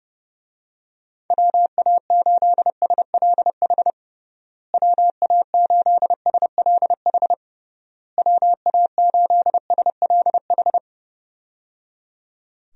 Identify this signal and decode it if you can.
Morse code, heard as a keyed tone, about 30 wpm, 700 Hz. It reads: WA8SLH WA8SLH WA8SLH
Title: USA Call Signs - 6 Characters - Repeated 3x - 30wpm